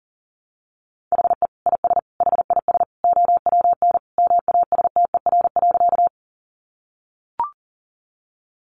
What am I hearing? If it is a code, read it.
Morse code, heard as a keyed tone, about 37 wpm, 700 Hz. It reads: HE IS HIS OWN MASTER.
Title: Sentences from Top 1300 Words 40wpm